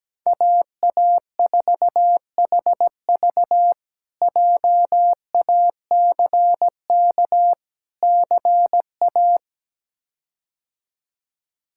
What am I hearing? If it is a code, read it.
Morse code, heard as a keyed tone, about 17 wpm, 700 Hz. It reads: AA4HV JACK CA
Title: K1USN SST QSO Practice - 17wpm